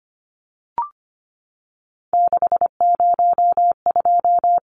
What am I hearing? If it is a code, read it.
Morse code, heard as a keyed tone, about 25 wpm, 700 Hz. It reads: 602
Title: Numbers - 3-Digit 25wpm